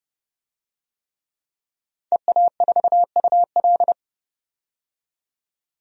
Morse code, heard as a keyed tone, about 30 wpm, 700 Hz. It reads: EA4UL
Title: DXCC-100 Call Signs - 5 Characters - 30wpm